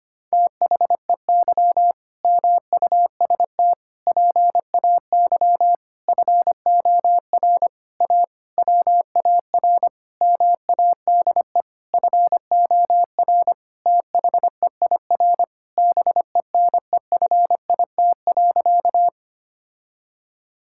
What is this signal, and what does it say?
Morse code, heard as a keyed tone, about 25 wpm, 700 Hz. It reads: THEY MUST PAY FOR A WAR MADE FOR THEIR BENEFIT.
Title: Sentences from Top 1000 Words - Review Part 6 - 25wpm